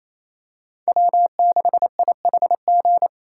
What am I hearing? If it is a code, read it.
Morse code, heard as a keyed tone, about 28 wpm, 700 Hz. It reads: W6IHG
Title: K1USN SST QSO Practice - 28wpm